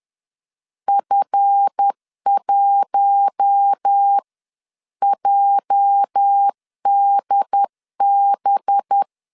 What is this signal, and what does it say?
Morse code, heard as a keyed tone, about 10 wpm, spaced as for 9 wpm, 800 Hz. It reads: F1JDB
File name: LEZIONI / 8 YJ45 / 8 YJ 10wpm 2651